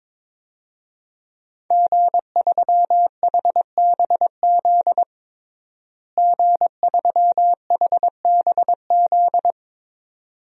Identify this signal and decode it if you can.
Morse code, heard as a keyed tone, about 22 wpm, 700 Hz. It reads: G3HBZ G3HBZ
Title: DXCC-100 Call Signs - 5 Characters - Repeated 2x - 22wpm